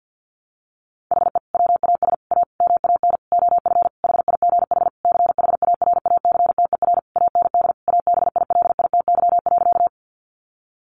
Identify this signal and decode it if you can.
Morse code, heard as a keyed tone, about 50 wpm, 700 Hz. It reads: HE WAS A MAN OF HIGH CHARACTER AND ABILITY.
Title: Sentences from Top 1200 Words 50wpm